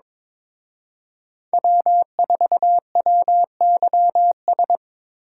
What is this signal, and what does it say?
Morse code, heard as a keyed tone, about 25 wpm, 700 Hz. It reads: W4WYS
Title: Speed Racing - USA 1x3 Call Signs 22wpm